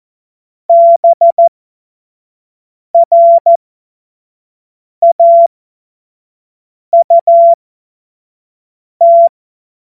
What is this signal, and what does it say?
Morse code, heard as a keyed tone, about 14 wpm, spaced as for 5 wpm, 680 Hz. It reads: BRAUT
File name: Morsekurs / Witze / Witz 04. Januar 2022, 25 F70